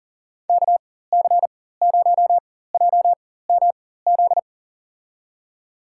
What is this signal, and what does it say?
Morse code, heard as a keyed tone, about 40 wpm, spaced as for 19 wpm, 700 Hz. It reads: KC0JMZ
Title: USA Call Signs - 4x Character Spacing - 40wpm